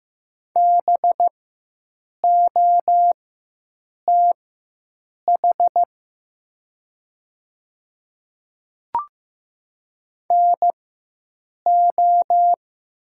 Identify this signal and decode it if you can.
Morse code, heard as a keyed tone, about 15 wpm, spaced as for 7 wpm, 700 Hz. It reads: BOTH NO
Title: 4x Character Spacing - Top 500 Words - 15wpm